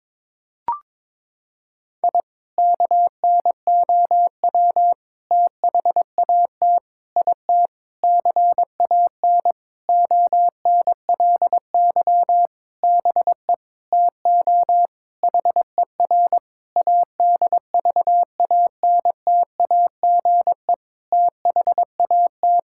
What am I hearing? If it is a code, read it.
Morse code, heard as a keyed tone, about 22 wpm, 700 Hz. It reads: I KNOW THAT IT CAN ONLY BE TO HER ADVANTAGE THAT
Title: Sentences from Top 1000 Words - Review Part 6 - 22wpm